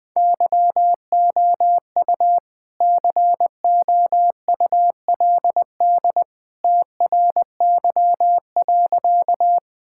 Morse code, heard as a keyed tone, about 20 wpm, 700 Hz. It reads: YOU COULD TRY.
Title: Sentences with 2-Word Phrases - 20wpm